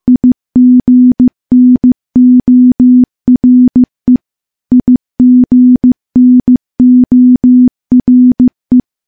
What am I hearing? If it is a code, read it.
Morse code, heard as a keyed tone, about 15 wpm, 250 Hz. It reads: IGNORE IGNORE
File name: morse-training-15WPM